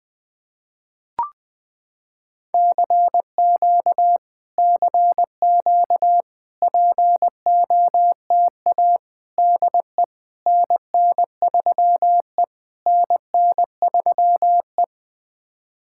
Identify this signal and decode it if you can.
Morse code, heard as a keyed tone, about 20 wpm, 700 Hz. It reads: CQ CQ POTA DE NN3E NN3E
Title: POTA QSO Practice - 20wpm